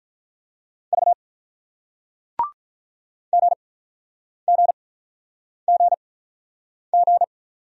Morse code, heard as a keyed tone, about 48 wpm, spaced as for 9 wpm, 700 Hz. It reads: V GGGG